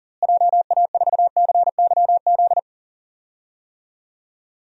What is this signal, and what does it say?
Morse code, heard as a keyed tone, about 40 wpm, 700 Hz. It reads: JA4CYZ
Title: DXCC-100 Call Signs - 6 Characters - Repeated 3x - 40wpm